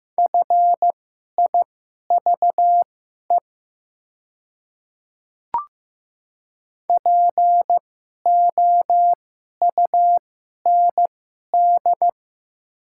Morse code, heard as a keyed tone, about 15 wpm, spaced as for 11 wpm, 700 Hz. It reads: FIVE POUND
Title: 2x Character Spacing - Top 500 Words - 15wpm